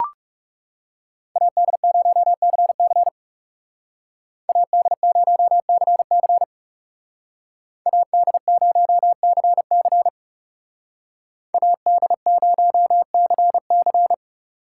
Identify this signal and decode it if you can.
Morse code, heard as a keyed tone, about 34 wpm, 700 Hz. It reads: AD0CC AD0CC AD0CC AD0CC
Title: Speed Racing - USA Call Signs 30wpm